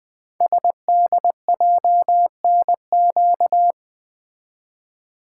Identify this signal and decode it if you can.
Morse code, heard as a keyed tone, about 20 wpm, 700 Hz. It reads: SDJNQ
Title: Code Groups - 5 Letter:Numbers - 20wpm